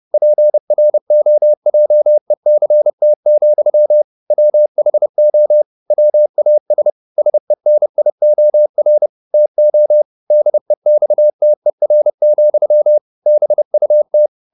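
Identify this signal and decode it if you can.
Morse code, heard as a keyed tone, about 30 wpm, 600 Hz. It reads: PROJECT, WHO WAS SENIOR TO DEXTER, BUT
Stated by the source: Morse code at 30 wpm, the novel Never by Ken Follett in English, from CW med Gnister.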